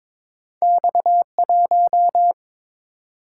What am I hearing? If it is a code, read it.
Morse code, heard as a keyed tone, about 22 wpm, 700 Hz. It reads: X 1
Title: Single-Letter Single-Number 22wpm